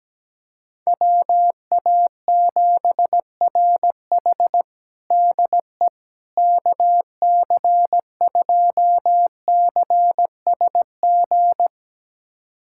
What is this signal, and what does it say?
Morse code, heard as a keyed tone, about 17 wpm, 700 Hz. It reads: WA7RH DE KC2CSG
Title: USA Callsign DE Callsign 17wpm